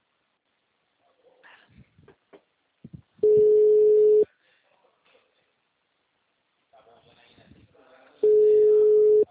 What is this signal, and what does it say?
Ringback tone (Europe, 425 Hz), with speech.
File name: out- -2045- - - .142